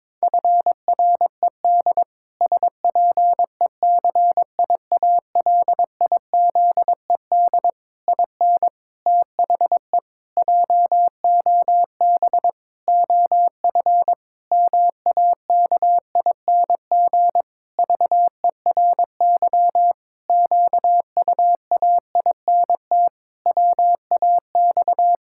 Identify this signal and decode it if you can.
Morse code, heard as a keyed tone, about 22 wpm, 700 Hz. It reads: FRED SPECIALIZED IN THE JOB OF MAKING VERY QUAINT WAX
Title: Pangram Challenge v2 - 22wpm